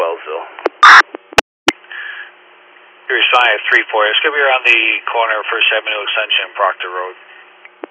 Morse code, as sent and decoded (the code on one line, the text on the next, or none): none